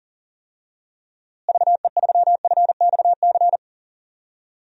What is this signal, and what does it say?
Morse code, heard as a keyed tone, about 40 wpm, 700 Hz. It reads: VE3FXC
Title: DXCC-100 Call Signs - 6 Characters - Repeated 3x - 40wpm